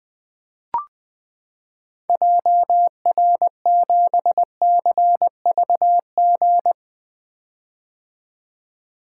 Morse code, heard as a keyed tone, about 20 wpm, 700 Hz. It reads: JR7CVG
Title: 2023 POTA Activator Call Signs - 20wpm